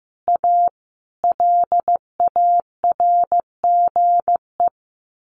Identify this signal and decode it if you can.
Morse code, heard as a keyed tone, about 15 wpm, 700 Hz. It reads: A LARGE
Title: Sentences from Top 1100 Words 15wpm